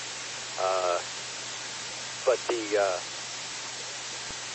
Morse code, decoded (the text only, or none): none